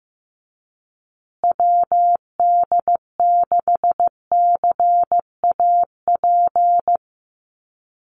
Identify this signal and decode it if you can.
Morse code, heard as a keyed tone, about 15 wpm, 700 Hz. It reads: WD6CAP
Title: USA Call Signs Repeated 2x 15wpm